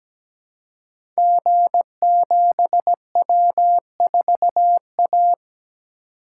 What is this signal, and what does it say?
Morse code, heard as a keyed tone, about 17 wpm, 700 Hz. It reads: G7W4A
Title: Code Groups - 5 Letter:Numbers - 17wpm